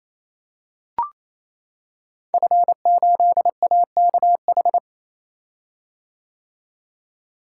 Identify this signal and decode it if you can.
Morse code, heard as a keyed tone, about 27 wpm, 700 Hz. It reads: F8AKH
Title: Code Groups - 5 Letter:Numbers - 28wpm